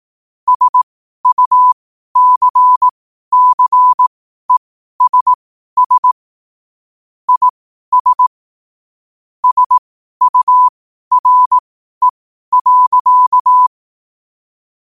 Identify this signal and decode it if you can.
Morse code, heard as a keyed tone, about 18 wpm, spaced as for 12 wpm, 1000 Hz. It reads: SUCCESS IS SURE.